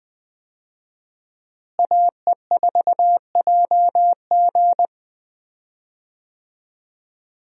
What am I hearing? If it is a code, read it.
Morse code, heard as a keyed tone, about 20 wpm, 700 Hz. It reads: AE4JG
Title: USA Call Signs - 5 Characters - Repeated 3x - 20wpm